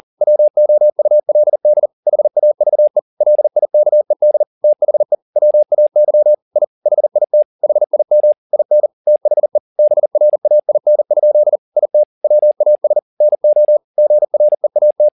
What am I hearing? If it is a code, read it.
Morse code, heard as a keyed tone, about 40 wpm, 600 Hz. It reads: WOULD HAVE LIKED THE WAY I HIT HIM IN THE BRAIN? IT WAS NO GREAT
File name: the old man and the sea 2 40wpm